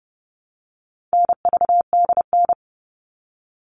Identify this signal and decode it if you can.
Morse code, heard as a keyed tone, about 30 wpm, 700 Hz. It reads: NVDN
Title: All Cut Numbers - 4-Digit 30wpm